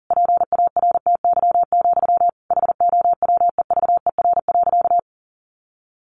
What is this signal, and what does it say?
Morse code, heard as a keyed tone, about 40 wpm, 700 Hz. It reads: PARTY, HOWEVER.
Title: Sentences from Top 1400 Words 40wpm